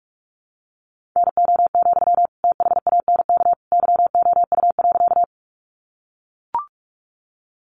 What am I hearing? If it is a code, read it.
Morse code, heard as a keyed tone, about 44 wpm, 700 Hz. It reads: NO, THANK YOU.
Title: Sentences from Top 600 Words 45wpm